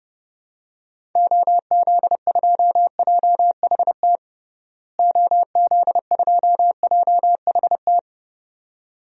Morse code, heard as a keyed tone, about 30 wpm, 700 Hz. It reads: OZ2JHT OZ2JHT
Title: DXCC-100 Call Signs - 6 Characters - Repeated 2x - 30wpm